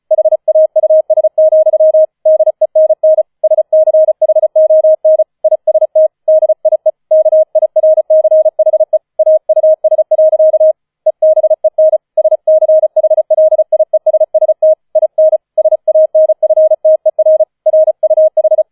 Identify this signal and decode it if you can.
Morse code, heard as a keyed tone, about 34 wpm, 630 Hz. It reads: HAUS, DENN SCHON IST DIE KIRCHE AUS. EBEN SCHLIESST IN SANFTER RUH
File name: 004 Vierter Streich 35wpm